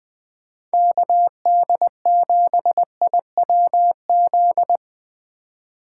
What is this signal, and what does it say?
Morse code, heard as a keyed tone, about 20 wpm, 700 Hz. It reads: KD7IWZ
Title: USA Call Signs - 6 Characters - 3x Character Spacing - 20wpm